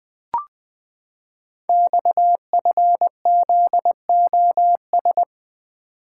Morse code, heard as a keyed tone, about 20 wpm, 700 Hz. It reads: XFZOS
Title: Code Groups - 5 Letters - 20wpm